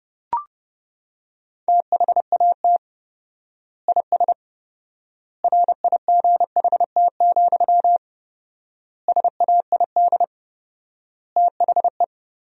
Morse code, heard as a keyed tone, about 30 wpm, 700 Hz. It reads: THAT IS RIGHT, SAID THE
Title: 4x Word-Spacing - Sentences from Top 200 Words - 30wpm